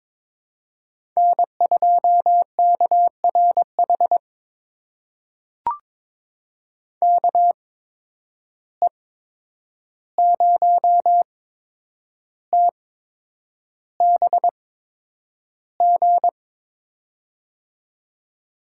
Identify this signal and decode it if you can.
Morse code, heard as a keyed tone, about 22 wpm, spaced as for 9 wpm, 700 Hz. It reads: N2KRH KE0TBG
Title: USA Call Signs - 8x Character Spacing - 22wpm